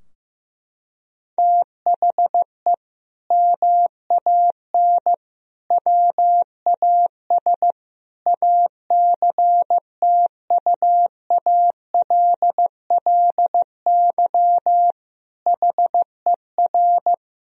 Morse code, heard as a keyed tone, about 15 wpm, 700 Hz. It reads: THE MAN WAS ACTUALLY HER
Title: Sentences from Top 1000 Words - Review Part 4 - 15wpm